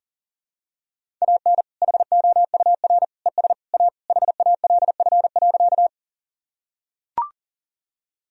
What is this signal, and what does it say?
Morse code, heard as a keyed tone, about 40 wpm, 700 Hz. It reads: AN HOUR ES A HALF.
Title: Sentences from Top 400 Words and Abbreviations 40wpm